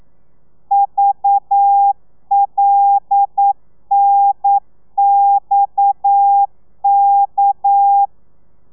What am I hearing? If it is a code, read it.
Morse code, heard as a keyed tone, about 9 wpm, 800 Hz. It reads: VLNXK
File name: telegrafia / i2tfj / Lez7 40